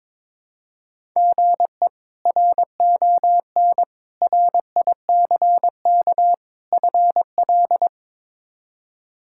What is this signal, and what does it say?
Morse code, heard as a keyed tone, about 22 wpm, 700 Hz. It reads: GE RON RICK FL
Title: K1USN SST QSO Practice - 22wpm